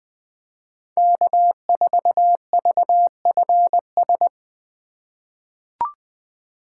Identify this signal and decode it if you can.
Morse code, heard as a keyed tone, about 21 wpm, 700 Hz. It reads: K4VFS
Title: USA Call Signs - 5 Characters - 4x Character Spacing - 20wpm